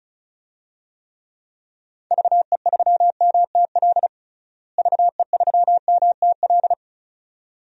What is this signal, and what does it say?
Morse code, heard as a keyed tone, about 36 wpm, 700 Hz. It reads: VE3MTL VE3MTL
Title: DXCC-100 Call Signs - 6 Characters - Repeated 2x - 35wpm